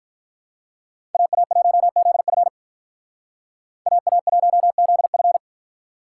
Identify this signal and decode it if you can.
Morse code, heard as a keyed tone, about 45 wpm, 700 Hz. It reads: AA1ZF AA1ZF